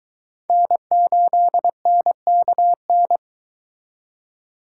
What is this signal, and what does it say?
Morse code, heard as a keyed tone, about 22 wpm, 700 Hz. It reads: N8NKN